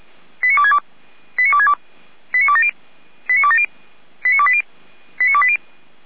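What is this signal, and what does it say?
Five-tone selective call (ZVEI), heard as quick runs of five tones, tones of about 70 ms. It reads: 89262 89262 89270 89270 89280 89280
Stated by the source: Five-tone ZVEI selective call received on an SDR.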